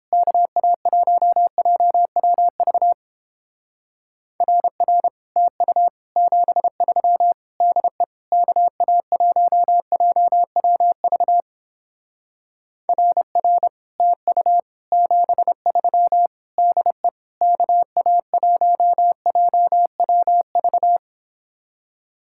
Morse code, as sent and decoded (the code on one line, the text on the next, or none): -.- .- .---- .--- .-- ...- / .-. .-. / - ..- / --... ...-- / -.. / -.- .- .---- .--- .-- ...- / .-. .-. / - ..- / --... ...-- / -.. / -.- .- .---- .--- .-- ...-
KA1JWV RR TU 73 DE KA1JWV RR TU 73 DE KA1JWV